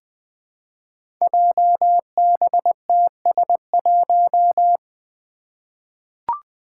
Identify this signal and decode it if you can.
Morse code, heard as a keyed tone, about 20 wpm, 700 Hz. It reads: JBTS1
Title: Five Challenging Letter-Number Combinations 20wpm